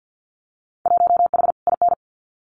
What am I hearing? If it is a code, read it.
Morse code, heard as a keyed tone, about 50 wpm, 700 Hz. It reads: 1H IN
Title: Winter Field Day Exchanges 50wpm